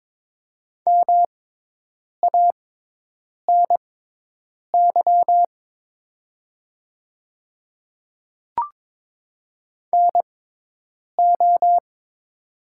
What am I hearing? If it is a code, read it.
Morse code, heard as a keyed tone, about 22 wpm, spaced as for 8 wpm, 700 Hz. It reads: MANY NO
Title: 6x Character Spacing - Top 200 Words - 22wpm